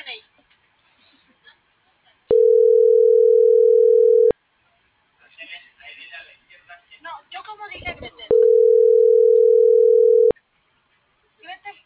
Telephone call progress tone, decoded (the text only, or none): ringback tone